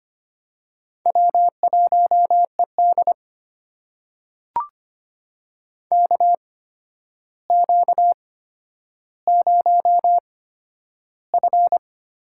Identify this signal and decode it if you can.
Morse code, heard as a keyed tone, about 25 wpm, spaced as for 11 wpm, 700 Hz. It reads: W1ED KQ0F
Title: USA Call Signs - 4 Characters - 8x Character Spacing - 25wpm